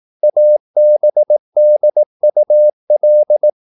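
Morse code, heard as a keyed tone, about 18 wpm, 600 Hz. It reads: ABDUL
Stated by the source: Morse practice at 18 wpm, English text, from CW med Gnister.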